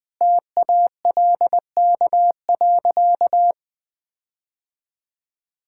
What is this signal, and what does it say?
Morse code, heard as a keyed tone, about 20 wpm, 700 Hz. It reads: TALK.
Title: Sentences with 4-Word Phrases - 20wpm